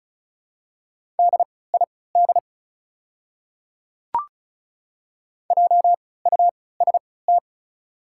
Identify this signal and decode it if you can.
Morse code, heard as a keyed tone, about 36 wpm, spaced as for 20 wpm, 700 Hz. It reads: DID JUST
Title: 3x Character Spacing - Top 200 Words - 35wpm